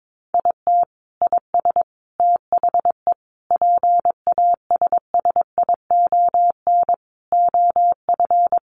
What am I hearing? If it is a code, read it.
Morse code, heard as a keyed tone, about 22 wpm, 700 Hz. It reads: IT IS THE PASSION OF
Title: Sentences from Top 1300 Words 22wpm